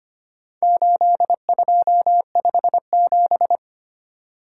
Morse code, heard as a keyed tone, about 25 wpm, 700 Hz. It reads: 8257
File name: practice / 25wpm / Four Letter-Number Combinations - 25wpm